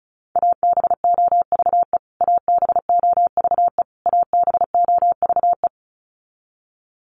Morse code, heard as a keyed tone, about 34 wpm, 700 Hz. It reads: ABOVE ABOVE ABOVE
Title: Top 500 Words Repeated 35wpm